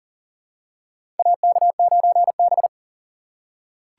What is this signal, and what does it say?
Morse code, heard as a keyed tone, about 40 wpm, 700 Hz. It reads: AK9B
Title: USA Call Signs - 4 Characters - 6x Character Spacing - 40wpm